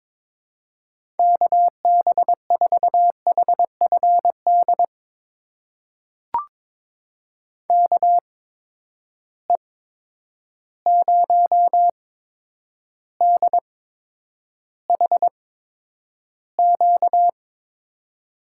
Morse code, heard as a keyed tone, about 22 wpm, spaced as for 9 wpm, 700 Hz. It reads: KB4HFD KE0DHQ
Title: USA Call Signs - 6 Characters - 8x Character Spacing - 22wpm